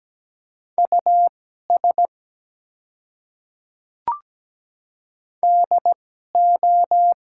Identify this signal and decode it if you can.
Morse code, heard as a keyed tone, about 17 wpm, spaced as for 12 wpm, 700 Hz. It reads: US DO
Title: 2x Character Spacing - Top 300 Words - 17wpm